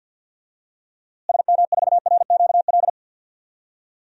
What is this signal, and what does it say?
Morse code, heard as a keyed tone, about 52 wpm, 700 Hz. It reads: SM4RQL
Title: DXCC-100 Call Signs - 6 Characters - Repeated 2x - 50wpm